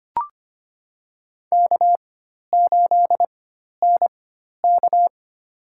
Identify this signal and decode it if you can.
Morse code, heard as a keyed tone, about 25 wpm, spaced as for 12 wpm, 700 Hz. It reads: K8NK
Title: USA Call Signs - 4x Character Spacing - 25wpm